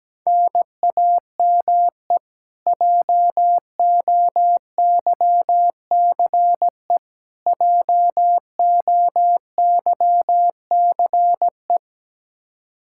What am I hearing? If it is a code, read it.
Morse code, heard as a keyed tone, about 17 wpm, 700 Hz. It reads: NAME JOYCE JOYCE
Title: Practice QSO - Exchange 1 - Beginning - 17wpm